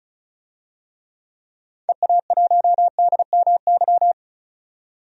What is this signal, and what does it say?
Morse code, heard as a keyed tone, about 36 wpm, 700 Hz. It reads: EA1DMY
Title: DXCC-100 Call Signs - 6 Characters - 35wpm